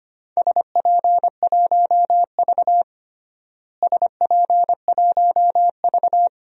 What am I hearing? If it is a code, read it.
Morse code, heard as a keyed tone, about 25 wpm, 700 Hz. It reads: SP1V SP1V
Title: DXCC-100 Call Signs - 4 Characters - Repeated 2x - 25wpm